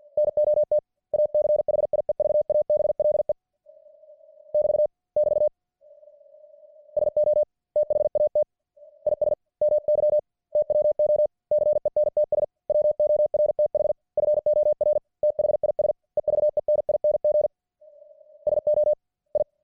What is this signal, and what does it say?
Morse code, heard as a keyed tone, about 50 wpm, 600 Hz. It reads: NOT ACHIEVABLE <BT> <BT> SO THAT IS MY TWO CENTS WORTH FOR THIS EVENING SO I